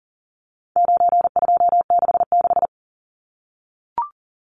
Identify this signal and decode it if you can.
Morse code, heard as a keyed tone, about 40 wpm, 700 Hz. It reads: 9266
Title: Numbers - 4-Digit 40wpm